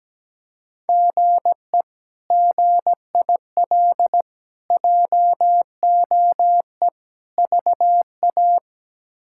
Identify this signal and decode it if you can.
Morse code, heard as a keyed tone, about 17 wpm, 700 Hz. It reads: GE GIL JOE VA
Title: K1USN SST QSO Practice - 17wpm